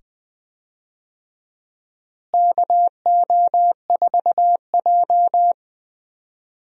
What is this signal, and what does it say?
Morse code, heard as a keyed tone, about 20 wpm, 700 Hz. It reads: KO4J
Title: USA Call Signs - 4 Characters - Repeated 2x - 20wpm